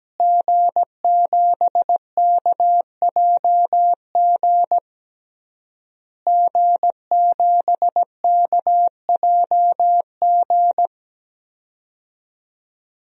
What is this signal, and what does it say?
Morse code, heard as a keyed tone, about 17 wpm, 700 Hz. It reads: G7KJG G7KJG
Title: DXCC-100 Call Signs - 5 Characters - Repeated 3x - 17wpm